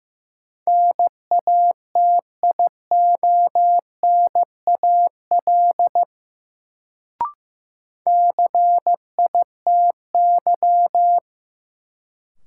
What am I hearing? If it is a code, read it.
Morse code, heard as a keyed tone, about 15 wpm, 700 Hz. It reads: NATIONAL CITY
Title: Sets of 2 Words - New Word Drill - 15wpm